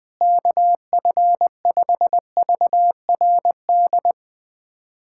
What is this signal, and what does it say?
Morse code, heard as a keyed tone, about 20 wpm, 700 Hz. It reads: KF5VRD
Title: K1USN SST QSO Practice - 20wpm